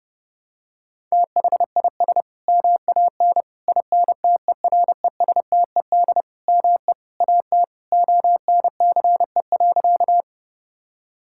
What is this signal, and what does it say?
Morse code, heard as a keyed tone, about 32 wpm, 700 Hz. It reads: THIS MAN INTERESTED ME AT ONCE.
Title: Sentences from Top 1000 Words - Review Part 6 - 30wpm